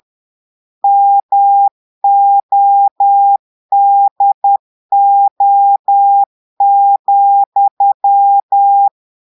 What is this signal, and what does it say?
Morse code, heard as a keyed tone, about 10 wpm, 800 Hz. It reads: MODO,
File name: portugais texte en clair 2 vitesse 10 mots minutes